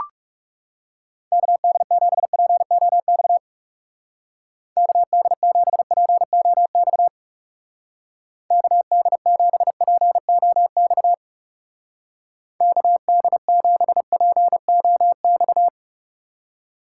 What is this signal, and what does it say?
Morse code, heard as a keyed tone, about 37 wpm, 700 Hz. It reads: KD7POX KD7POX KD7POX KD7POX
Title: Speed Racing - USA 2x3 Call Signs 30wpm